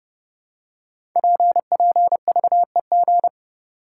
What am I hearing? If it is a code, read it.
Morse code, heard as a keyed tone, about 30 wpm, 700 Hz. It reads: PPVEG
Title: Code Groups - 5 Letter:Numbers - 30wpm